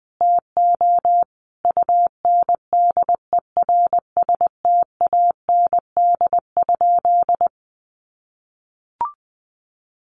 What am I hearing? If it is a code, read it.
Morse code, heard as a keyed tone, about 20 wpm, 700 Hz. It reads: TO UNDERSTAND?
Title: Sentences from Top 1100 Words 20wpm